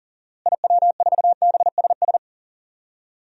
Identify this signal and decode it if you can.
Morse code, heard as a keyed tone, about 40 wpm, 700 Hz. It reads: IW4BSS
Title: DXCC-100 Call Signs - 6 Characters - Repeated 3x - 40wpm